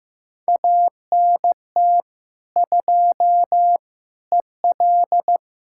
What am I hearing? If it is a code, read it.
Morse code, heard as a keyed tone, about 15 wpm, 700 Hz. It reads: ANT 2 EL
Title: Practice QSO - Exchange 2 - Info - 15wpm